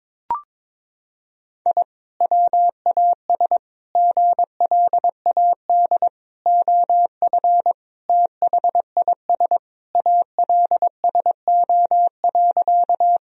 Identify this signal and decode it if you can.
Morse code, heard as a keyed tone, about 22 wpm, 700 Hz. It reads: I WAS GLAD OF THIS ALSO.
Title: Sentences from Top 1000 Words - Review Part 1 - 22wpm